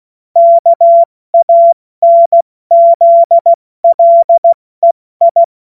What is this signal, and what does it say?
Morse code, heard as a keyed tone, about 16 wpm, 680 Hz. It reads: KANZLEI
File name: Morsekurs / Witze / Witz 10. Mai 2022 70 F80